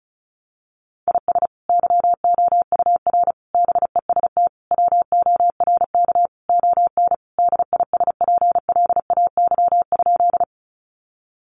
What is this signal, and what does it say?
Morse code, heard as a keyed tone, about 34 wpm, 700 Hz. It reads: IS YOUR BEST WORK ON DISPLAY?
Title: Sentences from Top 1200 Words 35wpm